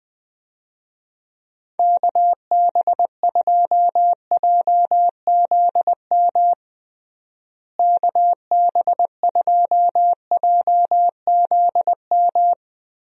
Morse code, heard as a keyed tone, about 21 wpm, 700 Hz. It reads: KB2JZM KB2JZM
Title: USA Call Signs - 6 Characters - Repeated 3x - 20wpm